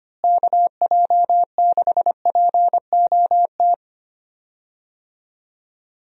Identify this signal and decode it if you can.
Morse code, heard as a keyed tone, about 25 wpm, 700 Hz. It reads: KJ6POT
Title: USA Call Signs - 6 Characters - Repeated 2x - 25wpm